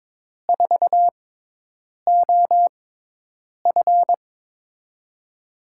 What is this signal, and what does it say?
Morse code, heard as a keyed tone, about 22 wpm, spaced as for 8 wpm, 700 Hz. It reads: 4OF
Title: USA Call Signs - 6x Character Spacing - 22wpm